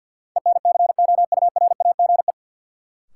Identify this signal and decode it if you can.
Morse code, heard as a keyed tone, about 53 wpm, 700 Hz. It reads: ENCOURAGE
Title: Sentences from Top 1400 Words - New Word Drill Alt - 50wpm